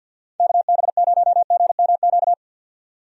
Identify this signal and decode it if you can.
Morse code, heard as a keyed tone, about 49 wpm, 700 Hz. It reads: KB0GMQ
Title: USA Call Signs - 6 Characters - 50wpm